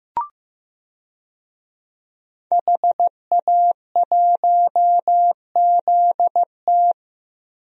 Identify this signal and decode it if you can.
Morse code, heard as a keyed tone, about 16 wpm, 700 Hz. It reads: HA1ZT
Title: DXCC-100 Call Signs - 5 Characters - Repeated 2x - 15wpm